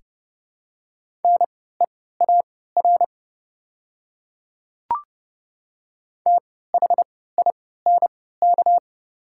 Morse code, heard as a keyed tone, about 29 wpm, spaced as for 17 wpm, 690 Hz. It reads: NEAR THINK
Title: 3x Character Spacing - Top 300 Words - 30wpm